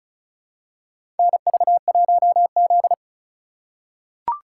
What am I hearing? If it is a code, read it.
Morse code, heard as a keyed tone, about 36 wpm, 700 Hz. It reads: NV1Z
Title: USA Call Signs - 3x Character Spacing - 35wpm